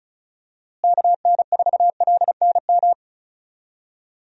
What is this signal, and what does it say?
Morse code, heard as a keyed tone, about 36 wpm, 700 Hz. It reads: KN4LNM
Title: USA Call Signs - 6 Characters - Repeated 2x - 35wpm